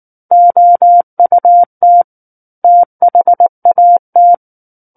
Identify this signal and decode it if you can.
Morse code, heard as a keyed tone, about 19 wpm, 700 Hz. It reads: OUT THAT
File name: Book One / 10-Chapter 10 - 19 WPM